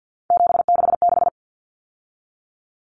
Morse code, heard as a keyed tone, about 50 wpm, 700 Hz. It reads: 766
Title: Common Cut Numbers - 3-Digit 50wpm